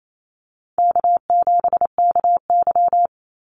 Morse code, heard as a keyed tone, about 28 wpm, 700 Hz. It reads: K7KY
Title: Easy Call Signs Repeated 2x - 28wpm